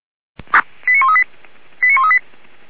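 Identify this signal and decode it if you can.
Five-tone selective call (ZVEI), heard as quick runs of five tones, tones of about 80 ms. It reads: 89138 89138
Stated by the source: ZVEI five-tone selective call received on an SDR.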